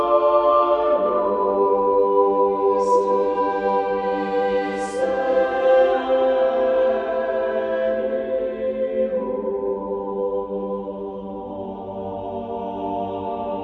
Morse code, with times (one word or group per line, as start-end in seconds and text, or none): none